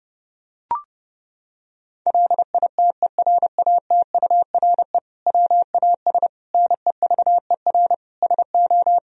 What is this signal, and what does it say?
Morse code, heard as a keyed tone, about 30 wpm, 700 Hz. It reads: LITERATURE WAS NEVER SO
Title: Sentences from Top 1000 Words - Review Part 3 - 30wpm